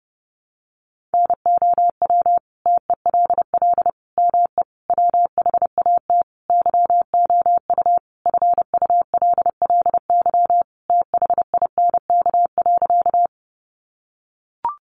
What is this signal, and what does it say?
Morse code, heard as a keyed tone, about 28 wpm, 700 Hz. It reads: NOW TELL ME WHAT YOU FULLY THINK.
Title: Sentences from Top 1300 Words 30wpm